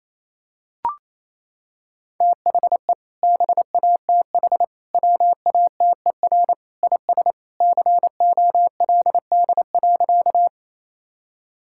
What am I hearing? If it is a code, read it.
Morse code, heard as a keyed tone, about 28 wpm, 700 Hz. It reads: THE BATH WATER IS COLD.
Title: Sentences from Top 1600 Words 28wpm